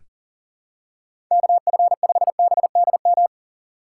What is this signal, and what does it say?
Morse code, heard as a keyed tone, about 40 wpm, 700 Hz. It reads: KF5BDM
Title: USA Call Signs - 6 Characters - 3x Character Spacing - 40wpm